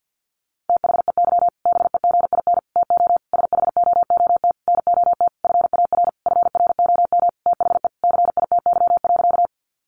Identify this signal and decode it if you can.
Morse code, heard as a keyed tone, about 50 wpm, 700 Hz. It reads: THEY BEGIN TO SHOOT NOT FAR FROM THE CITY.
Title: Sentences from Top 1200 Words 50wpm